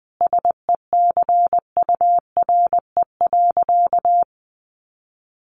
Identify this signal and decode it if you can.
Morse code, heard as a keyed tone, about 20 wpm, 700 Hz. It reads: SECURE.
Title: Sentences from Top 1200 Words 20wpm